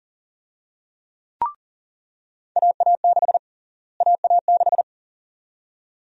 Morse code, heard as a keyed tone, about 40 wpm, 700 Hz. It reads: AA6 AA6
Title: USA Call Sign Prefixes - Repeated 2x - 40wpm